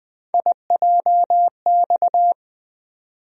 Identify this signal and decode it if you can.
Morse code, heard as a keyed tone, about 20 wpm, 700 Hz. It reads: IJX
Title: USA Call Sign Postfixes - 3 Letters 20wpm